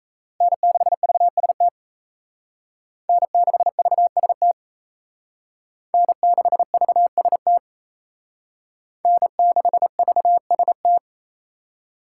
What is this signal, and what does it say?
Morse code, heard as a keyed tone, about 35 wpm, 700 Hz. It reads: N6VST N6VST N6VST N6VST
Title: Speed Racing - USA 1x3 Call Signs 28wpm